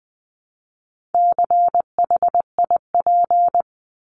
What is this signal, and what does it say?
Morse code, heard as a keyed tone, about 20 wpm, 700 Hz. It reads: CHIP
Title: Top 1751-2000 Words 20wpm